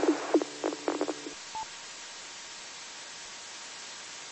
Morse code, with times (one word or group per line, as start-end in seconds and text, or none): none